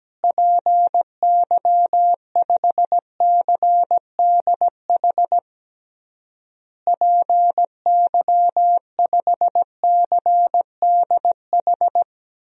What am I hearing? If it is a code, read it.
Morse code, heard as a keyed tone, about 17 wpm, 700 Hz. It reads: PY5CDH PY5CDH
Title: DXCC-100 Call Signs - 6 Characters - Repeated 3x - 17wpm